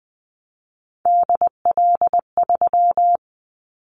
Morse code, heard as a keyed tone, about 20 wpm, 700 Hz. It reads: DL3
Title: Callsign Prefixes for Top 50 DXCC - 20wpm